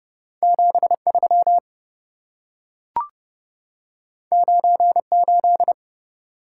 Morse code, heard as a keyed tone, about 27 wpm, 700 Hz. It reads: 73 98